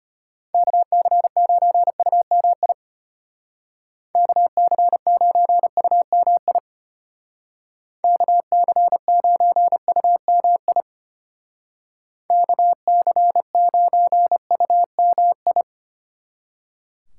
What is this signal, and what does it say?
Morse code, heard as a keyed tone, about 30 wpm, 700 Hz. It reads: KC9UMI KC9UMI KC9UMI KC9UMI
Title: Speed Racing - USA 2x3 Call Signs 25wpm